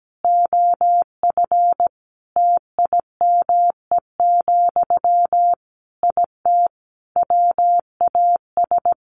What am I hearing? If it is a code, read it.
Morse code, heard as a keyed tone, about 17 wpm, 700 Hz. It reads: OF TIME, IT WAS
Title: Sentences from Top 1300 Words 17wpm